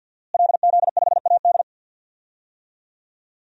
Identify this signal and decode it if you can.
Morse code, heard as a keyed tone, about 51 wpm, 700 Hz. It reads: LZ5AD
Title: DXCC-100 Call Signs - 5 Characters - Repeated 2x - 50wpm